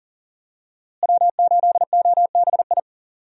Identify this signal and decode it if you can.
Morse code, heard as a keyed tone, about 37 wpm, 700 Hz. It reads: W8OBI
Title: USA Call Signs - 5 Characters - Repeated 2x - 40wpm